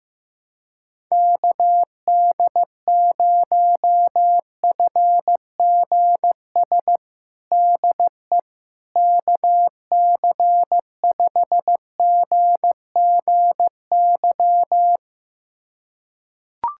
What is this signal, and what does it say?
Morse code, heard as a keyed tone, about 15 wpm, 700 Hz. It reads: KD0FGS DE KC5GGY
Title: USA Callsign DE Callsign - 6 Characters - 15wpm